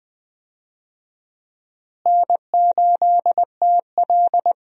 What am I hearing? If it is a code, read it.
Morse code, heard as a keyed tone, about 21 wpm, 700 Hz. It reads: N8TL
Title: USA Call Signs - 4 Characters - Repeated 3x - 20wpm